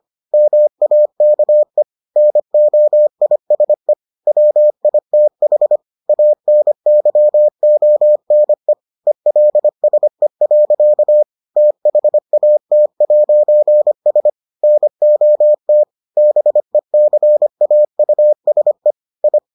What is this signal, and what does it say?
Morse code, heard as a keyed tone, about 25 wpm, 600 Hz. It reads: MAKE NOISE WITH ANYONE ELSE. THAT'S NOT BECAUSE I